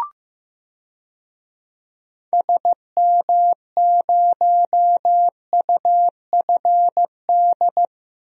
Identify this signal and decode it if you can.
Morse code, heard as a keyed tone, about 15 wpm, 700 Hz. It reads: SM0UFD
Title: DXCC-100 Call Signs - 6 Characters - Repeated 3x - 15wpm